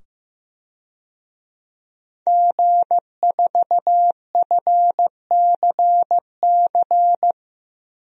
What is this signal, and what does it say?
Morse code, heard as a keyed tone, about 15 wpm, 700 Hz. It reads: G4FCC
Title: DXCC-100 Call Signs - 5 Characters - 15wpm